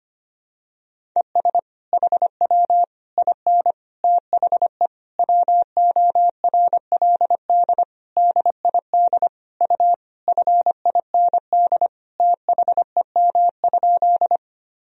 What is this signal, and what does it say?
Morse code, heard as a keyed tone, about 25 wpm, 700 Hz. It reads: ES HW IN THE WORLD DID U FIND THEM?
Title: Sentences from Top 200 Words and Abbreviations 25wpm